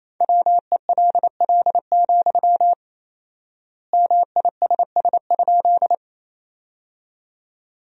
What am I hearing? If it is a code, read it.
Morse code, heard as a keyed tone, about 28 wpm, 700 Hz. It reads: WELL, MISS?
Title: 4x Word-Spacing - Sentences from Top 500 Words - 28wpm